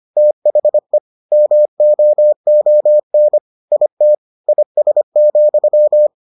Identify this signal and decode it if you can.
Morse code, heard as a keyed tone, about 25 wpm, 600 Hz. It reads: THE MOON IT IS,